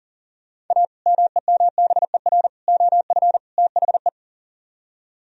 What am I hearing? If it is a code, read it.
Morse code, heard as a keyed tone, about 50 wpm, 700 Hz. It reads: A MEMBER OF THE
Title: Speed-Racing - Sets of 4 Words 40wpm